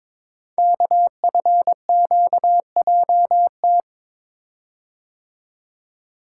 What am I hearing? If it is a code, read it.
Morse code, heard as a keyed tone, about 22 wpm, 700 Hz. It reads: KFQJT
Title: Code Groups - 5 Letters - 22wpm